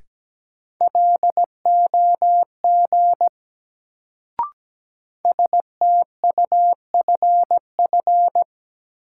Morse code, heard as a keyed tone, about 17 wpm, 700 Hz. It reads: LOG STUFF
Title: Sentences from Top 1400 Words - New Word Drill - 17wpm